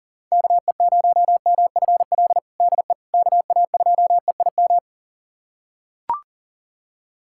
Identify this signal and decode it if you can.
Morse code, heard as a keyed tone, about 40 wpm, 700 Hz. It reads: KE0MFL DE KA2EIM
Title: USA Callsign DE Callsign - 6 Characters - 40wpm